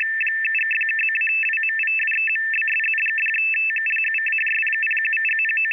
Two-tone teleprinter signal, about 850 Hz shift.
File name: F-NY 50Bd